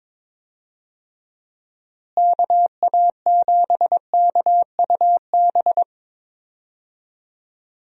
Morse code, heard as a keyed tone, about 22 wpm, 700 Hz. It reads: KA7KUB
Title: USA Call Signs - 6 Characters - Repeated 2x - 22wpm